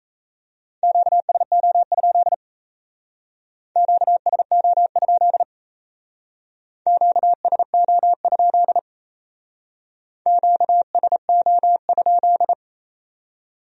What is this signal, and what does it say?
Morse code, heard as a keyed tone, about 35 wpm, 700 Hz. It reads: QSO? QSO? QSO? QSO?
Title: Speed-Racing Q-Codes - 28wpm